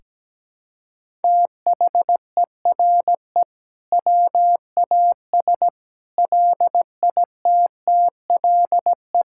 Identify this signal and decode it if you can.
Morse code, heard as a keyed tone, about 17 wpm, 700 Hz. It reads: THERE WAS LITTLE
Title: Sentences from Top 1000 Words - Review Part 6 - 17wpm